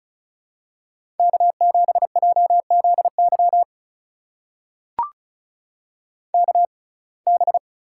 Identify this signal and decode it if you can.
Morse code, heard as a keyed tone, about 36 wpm, spaced as for 25 wpm, 700 Hz. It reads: K7JZY KB
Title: USA Call Signs - 5 Characters - 6x Character Spacing - 35wpm